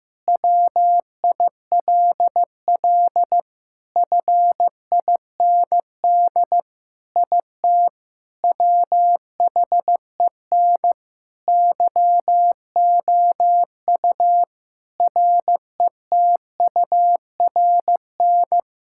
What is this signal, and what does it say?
Morse code, heard as a keyed tone, about 15 wpm, 700 Hz. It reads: WILL FIND IT WHEN YOU RETURN
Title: Sentences from Top 1600 Words 15wpm